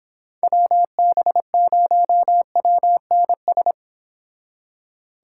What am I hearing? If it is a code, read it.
Morse code, heard as a keyed tone, about 29 wpm, 700 Hz. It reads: WB0WNS